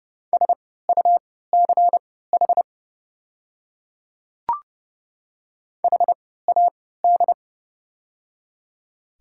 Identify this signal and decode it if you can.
Morse code, heard as a keyed tone, about 31 wpm, spaced as for 17 wpm, 690 Hz. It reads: SUCH HAD
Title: 3x Character Spacing - Top 200 Words - 30wpm